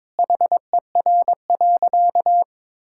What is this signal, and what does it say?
Morse code, heard as a keyed tone, about 21 wpm, 700 Hz. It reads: HER.
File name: practice / 22wpm / Sentences with 4-Word Phrases - 22wpm